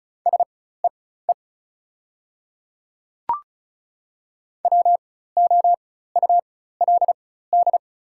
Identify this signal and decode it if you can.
Morse code, heard as a keyed tone, about 34 wpm, spaced as for 16 wpm, 700 Hz. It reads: SEE WOULD
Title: 4x Character Spacing - Top 200 Words - 35wpm